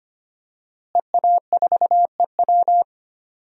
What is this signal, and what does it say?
Morse code, heard as a keyed tone, about 25 wpm, 700 Hz. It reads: EA4EW
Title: DXCC-100 Call Signs - 5 Characters - Repeated 2x - 25wpm